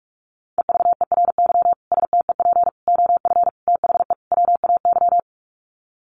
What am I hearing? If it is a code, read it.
Morse code, heard as a keyed tone, about 44 wpm, 700 Hz. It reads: EVERY STEP OF THE WAY
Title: Sets of 5 Words Encore2 45wpm